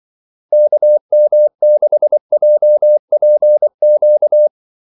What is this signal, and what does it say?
Morse code, heard as a keyed tone, about 24 wpm, 600 Hz. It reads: KM6JPQ
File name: Callsigns-24wpm